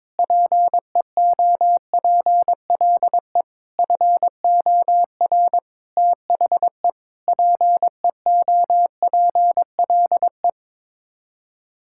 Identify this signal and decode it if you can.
Morse code, heard as a keyed tone, about 21 wpm, 700 Hz. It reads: PEOPLE FOR THE PEOPLE
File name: Sets of 9 Words - 22wpm